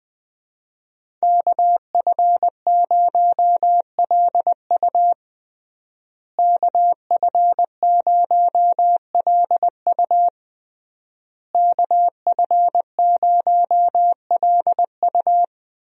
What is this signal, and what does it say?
Morse code, heard as a keyed tone, about 20 wpm, 700 Hz. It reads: KF0LU KF0LU KF0LU
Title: USA Call Signs - 5 Characters - Repeated 3x - 20wpm